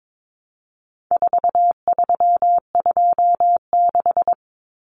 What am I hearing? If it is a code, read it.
Morse code, heard as a keyed tone, about 22 wpm, 700 Hz. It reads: 4326
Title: Common Cut Numbers - 4-Digit 22wpm